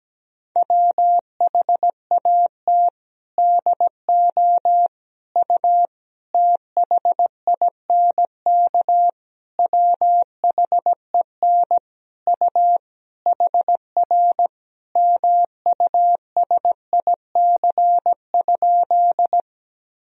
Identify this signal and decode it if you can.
Morse code, heard as a keyed tone, about 17 wpm, 700 Hz. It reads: WHAT DO U THINK WHEN U HR MUSIC?
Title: Sentences from Top 300 Words and Abbreviations 17wpm